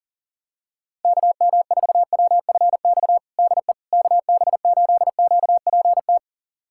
Morse code, heard as a keyed tone, about 40 wpm, 700 Hz. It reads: KM4WFX DE KB8QPT
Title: USA Callsign DE Callsign - 6 Characters - 40wpm